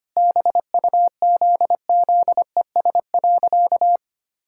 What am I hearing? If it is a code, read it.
Morse code, heard as a keyed tone, about 25 wpm, 700 Hz. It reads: BUZZES.
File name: practice / 25wpm / Sentences with 4-Word Phrases - 25wpm